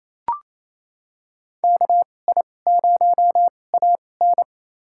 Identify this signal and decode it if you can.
Morse code, heard as a keyed tone, about 28 wpm, spaced as for 20 wpm, 700 Hz. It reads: KI0AN
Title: USA Call Signs - 5 Characters - 2x Character Spacing - 28wpm